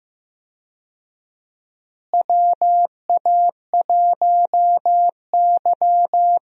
Morse code, heard as a keyed tone, about 16 wpm, 700 Hz. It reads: WA1Y
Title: USA Call Signs - 4 Characters - Repeated 2x - 15wpm